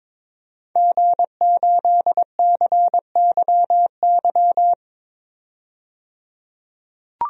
Morse code, heard as a keyed tone, about 22 wpm, 700 Hz. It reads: G8CYY
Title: DXCC-100 Call Signs - 5 Characters - 22wpm